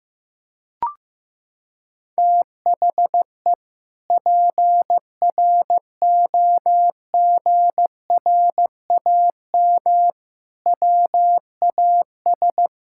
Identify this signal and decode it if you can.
Morse code, heard as a keyed tone, about 15 wpm, 700 Hz. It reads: THE PROGRAM WAS
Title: Sentences from Top 1000 Words - Review Part 5 - 15wpm